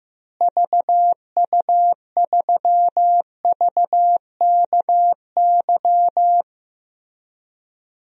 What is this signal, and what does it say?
Morse code, heard as a keyed tone, about 15 wpm, 700 Hz. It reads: VU3VKY
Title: DXCC-100 Call Signs - 6 Characters - Repeated 3x - 15wpm